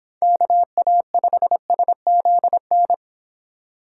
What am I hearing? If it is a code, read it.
Morse code, heard as a keyed tone, about 29 wpm, 700 Hz. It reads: KA5SZN